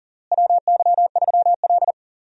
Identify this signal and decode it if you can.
Morse code, heard as a keyed tone, about 40 wpm, 700 Hz. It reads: WY3L
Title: USA Call Signs - 4 Characters - Repeated 3x - 40wpm